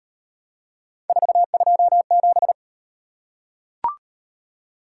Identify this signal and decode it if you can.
Morse code, heard as a keyed tone, about 34 wpm, 700 Hz. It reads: 427